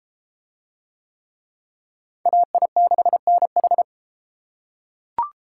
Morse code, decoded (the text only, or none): AI6NH